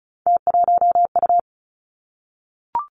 Morse code, heard as a keyed tone, about 36 wpm, 700 Hz. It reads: T1U
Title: Three Challenging Letter-Number Combinations 35wpm